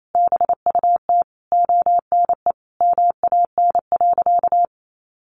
Morse code, heard as a keyed tone, about 28 wpm, 700 Hz. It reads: BUT ONE MAN.
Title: Sentences from Top 1500 Words 28wpm